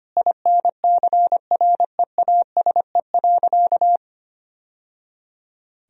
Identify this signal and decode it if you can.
Morse code, heard as a keyed tone, about 25 wpm, 700 Hz. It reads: INCREASE.
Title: Sentences from Top 1000 Words - Review Part 4 - 25wpm